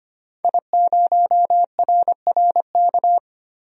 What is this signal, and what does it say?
Morse code, heard as a keyed tone, about 25 wpm, 700 Hz. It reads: I0RRK
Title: DXCC-100 Call Signs - 5 Characters - 25wpm